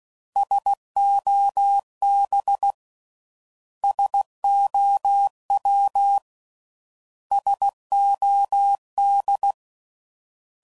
Morse code, heard as a keyed tone, about 16 wpm, 800 Hz. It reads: SOB SOW SOD